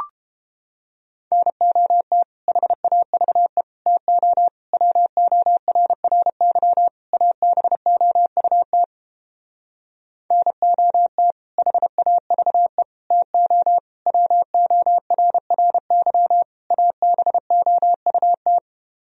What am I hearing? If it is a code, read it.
Morse code, heard as a keyed tone, about 31 wpm, 700 Hz. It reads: NOT HAVE TO WORRY ABOUT NOT HAVE TO WORRY ABOUT